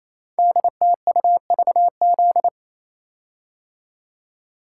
Morse code, heard as a keyed tone, about 27 wpm, 700 Hz. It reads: DTUVZ
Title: Code Groups - 5 Letters - 28wpm